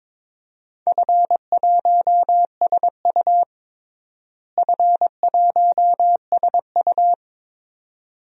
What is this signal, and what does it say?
Morse code, heard as a keyed tone, about 22 wpm, 700 Hz. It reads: F1SU F1SU
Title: DXCC-100 Call Signs - 4 Characters - Repeated 2x - 22wpm